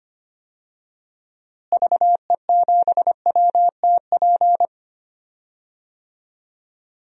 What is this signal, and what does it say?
Morse code, heard as a keyed tone, about 25 wpm, 700 Hz. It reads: VE7WTP
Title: DXCC-100 Call Signs - 6 Characters - Repeated 3x - 25wpm